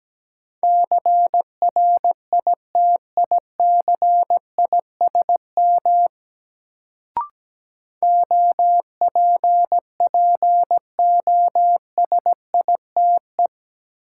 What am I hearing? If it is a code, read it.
Morse code, heard as a keyed tone, about 17 wpm, 700 Hz. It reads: CRITICISM OPPOSITE
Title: Sentences from Top 1300 Words - New Word Drill - 17wpm